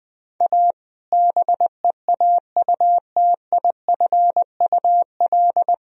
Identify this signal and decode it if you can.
Morse code, heard as a keyed tone, about 20 wpm, 700 Hz. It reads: A BEAUTIFUL
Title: Sentences from Top 1000 Words - Review Part 2 - 20wpm